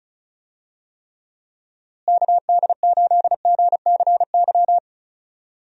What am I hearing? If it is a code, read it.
Morse code, heard as a keyed tone, about 36 wpm, 700 Hz. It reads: KD8GCY
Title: USA Call Signs - 6 Characters - Repeated 3x - 35wpm